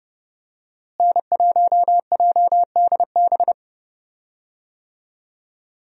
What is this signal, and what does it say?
Morse code, heard as a keyed tone, about 30 wpm, 700 Hz. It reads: N1JDB
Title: USA Call Signs - 5 Characters - Repeated 2x - 30wpm